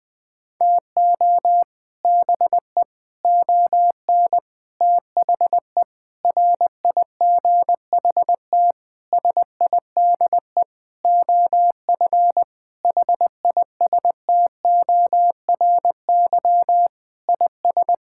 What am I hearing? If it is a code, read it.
Morse code, heard as a keyed tone, about 20 wpm, 700 Hz. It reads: TO BE ON THE RIGHT SIDE OF HISTORY IS
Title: Sentences with 8-Word Phrases - 20wpm